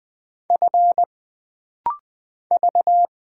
Morse code, heard as a keyed tone, about 20 wpm, 700 Hz. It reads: F V
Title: Single Letter-Number - Rapid-Fire 20wpm